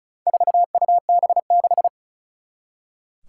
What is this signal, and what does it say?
Morse code, heard as a keyed tone, about 36 wpm, 700 Hz. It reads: 4UB6
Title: Four Challenging Letter-Number Combinations 35wpm